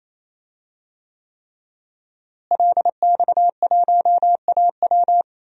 Morse code, heard as a keyed tone, about 28 wpm, 700 Hz. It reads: LX1AW
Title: DXCC-100 Call Signs - 5 Characters - Repeated 3x - 28wpm